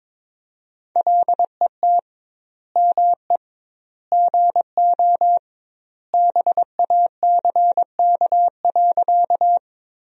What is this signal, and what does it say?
Morse code, heard as a keyed tone, about 22 wpm, 700 Hz. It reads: LET ME GO BACK.
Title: 2x Word-Spacing - Sentences from Top 300 Words - 22wpm